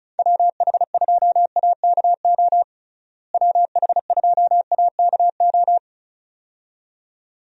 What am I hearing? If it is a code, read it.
Morse code, heard as a keyed tone, about 34 wpm, 700 Hz. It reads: WH2AKO WH2AKO
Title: USA Call Signs - 6 Characters - Repeated 2x - 35wpm